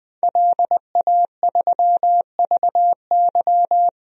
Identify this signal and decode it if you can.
Morse code, heard as a keyed tone, about 20 wpm, 700 Hz. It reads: LA3VY
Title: DXCC-100 Call Signs - 5 Characters - 20wpm